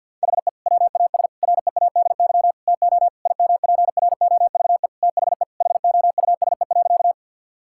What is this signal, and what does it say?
Morse code, heard as a keyed tone, about 50 wpm, 700 Hz. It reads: HE WAS READY TO IMPROVE THE HOUSE.
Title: Sentences from Top 1000 Words - Review Part 5 - 50wpm